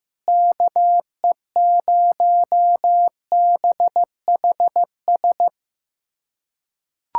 Morse code, heard as a keyed tone, about 15 wpm, 700 Hz. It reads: KE0BHS
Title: USA Callsign DE Callsign - 6 Characters - 15wpm